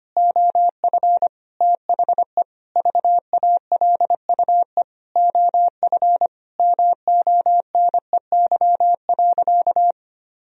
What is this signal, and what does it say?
Morse code, heard as a keyed tone, about 25 wpm, 700 Hz. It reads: OF THE VALUE OF MONEY.
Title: Sentences from Top 1000 Words - Review Part 3 - 25wpm